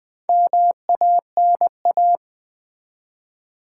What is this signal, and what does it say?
Morse code, heard as a keyed tone, about 20 wpm, 700 Hz. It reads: MANA
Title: Build A Word - 4 Letters At A Time - 10-Letter Words - 20wpm